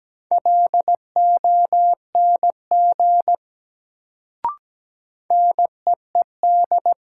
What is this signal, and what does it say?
Morse code, heard as a keyed tone, about 17 wpm, 700 Hz. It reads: LONG NEED
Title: Sets of 2 Words - New Word Drill - 17wpm